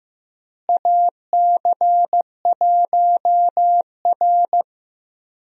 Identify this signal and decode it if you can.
Morse code, heard as a keyed tone, about 15 wpm, 700 Hz. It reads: AC1R
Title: USA Call Signs - 4 Characters - Repeated 3x - 15wpm